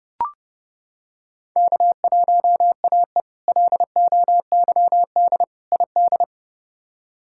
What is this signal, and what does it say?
Morse code, heard as a keyed tone, about 30 wpm, 700 Hz. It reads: K1AE LOYD ID
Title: K1USN SST QSO Practice - 30wpm